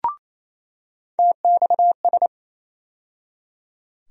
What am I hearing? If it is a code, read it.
Morse code, heard as a keyed tone, about 28 wpm, 700 Hz. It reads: TXS
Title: USA Call Sign Postfixes - 3 Letters 28wpm